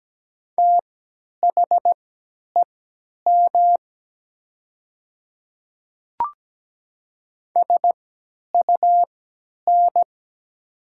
Morse code, heard as a keyed tone, about 17 wpm, spaced as for 10 wpm, 700 Hz. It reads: THEM SUN
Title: 3x Character Spacing - Top 500 Words - 17wpm